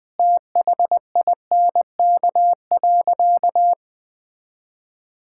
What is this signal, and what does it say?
Morse code, heard as a keyed tone, about 20 wpm, 700 Hz. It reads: THINK.
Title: Sentences with 5-Word Phrases - 20wpm